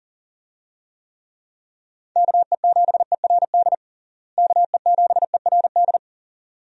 Morse code, heard as a keyed tone, about 37 wpm, 700 Hz. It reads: KE7ERD KE7ERD
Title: USA Call Signs - 6 Characters - Repeated 2x - 40wpm